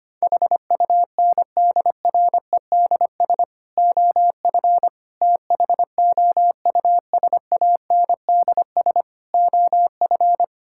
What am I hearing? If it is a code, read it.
Morse code, heard as a keyed tone, about 25 wpm, 700 Hz. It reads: HUNDREDS OF THOUSANDS OF
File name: practice / 25wpm / Sentences with 4-Word Phrases - 25wpm